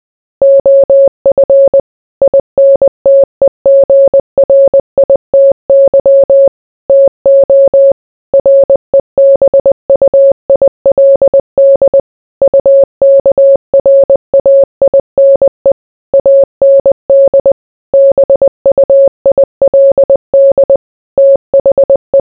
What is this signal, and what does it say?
Morse code, heard as a keyed tone, about 20 wpm, 550 Hz. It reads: OF INTEGRITY TO REBUILD UKRAINE AND BUILD THE